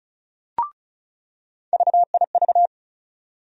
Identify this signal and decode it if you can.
Morse code, heard as a keyed tone, about 36 wpm, 700 Hz. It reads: VIV
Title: USA Call Sign Postfixes - 3 Letters 35wpm